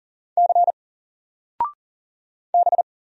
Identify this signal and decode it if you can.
Morse code, heard as a keyed tone, about 40 wpm, 700 Hz. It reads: C B
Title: Single Letter-Number 40wpm